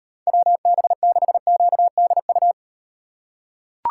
Morse code, heard as a keyed tone, about 38 wpm, 700 Hz. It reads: WB6QDU